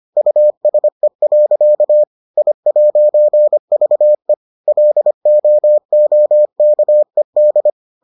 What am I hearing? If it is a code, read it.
Morse code, heard as a keyed tone, about 25 wpm, 600 Hz. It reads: USE. I'VE LOOKED